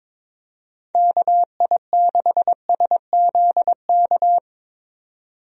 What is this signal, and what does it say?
Morse code, heard as a keyed tone, about 22 wpm, 700 Hz. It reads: KI6SZK
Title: USA Call Signs - 6 Characters - 22wpm